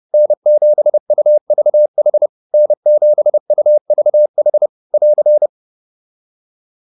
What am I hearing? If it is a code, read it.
Morse code, heard as a keyed tone, about 30 wpm, 600 Hz. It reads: N7UVH N7UVH <AR>